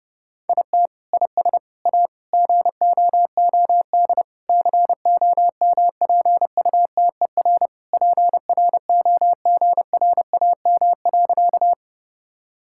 Morse code, heard as a keyed tone, about 30 wpm, 700 Hz. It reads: IT IS A GOOD COMPUTER PROGRAM.
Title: Sentences from Top 1000 Words - Review Part 5 - 30wpm